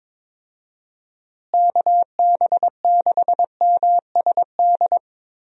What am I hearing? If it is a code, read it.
Morse code, heard as a keyed tone, about 22 wpm, 700 Hz. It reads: KB6MSD
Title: USA Call Signs - 6 Characters - Repeated 3x - 22wpm